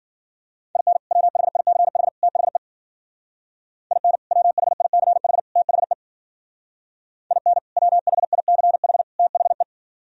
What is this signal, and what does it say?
Morse code, heard as a keyed tone, about 49 wpm, 700 Hz. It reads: IN WHICH THE IN WHICH THE IN WHICH THE
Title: Speed-Racing - Sets of 3 Words 40wpm